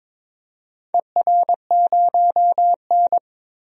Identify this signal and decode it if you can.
Morse code, heard as a keyed tone, about 21 wpm, 700 Hz. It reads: ER0N
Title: DXCC-100 Call Signs - 4 Characters - 22wpm